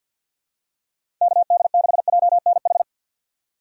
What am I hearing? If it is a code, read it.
Morse code, heard as a keyed tone, about 50 wpm, 700 Hz. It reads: KD6JNH
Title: USA Call Signs - 6 Characters - 50wpm